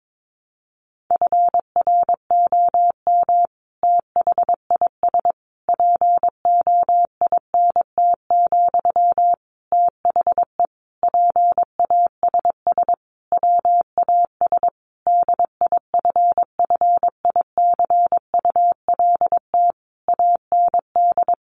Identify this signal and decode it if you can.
Morse code, heard as a keyed tone, about 22 wpm, 700 Hz. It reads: FROM THIS POINT, THE PASS WAS DIFFICULT AND
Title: Sentences from Top 1200 Words 22wpm